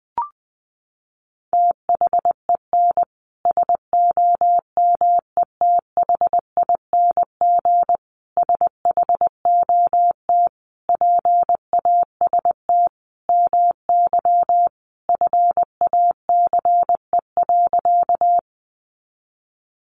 Morse code, heard as a keyed tone, about 20 wpm, 700 Hz. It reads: THEN SOMETHING SHOT PAST MY FACE.
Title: Sentences from Top 1100 Words 20wpm